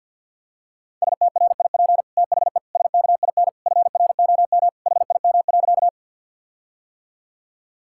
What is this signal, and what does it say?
Morse code, heard as a keyed tone, about 52 wpm, 700 Hz. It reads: STRIP THE SKIN FROM HIM.
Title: Sentences from Top 1600 Words 50wpm